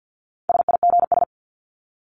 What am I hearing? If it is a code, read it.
Morse code, heard as a keyed tone, about 50 wpm, 700 Hz. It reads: SIGS
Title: Abbreviations 50wpm